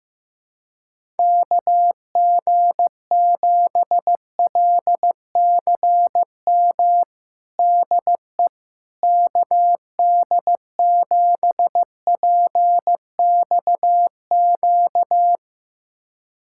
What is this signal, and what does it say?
Morse code, heard as a keyed tone, about 15 wpm, 700 Hz. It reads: KG7LCM DE KD7PXQ
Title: USA Callsign DE Callsign - 6 Characters - 15wpm